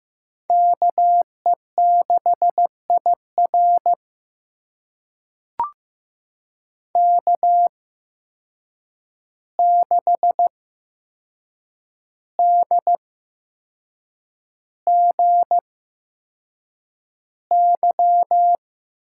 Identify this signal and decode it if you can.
Morse code, heard as a keyed tone, about 15 wpm, spaced as for 6 wpm, 700 Hz. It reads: KE6IR K6DGY
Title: USA Call Signs - 5 Characters - 8x Character Spacing - 15wpm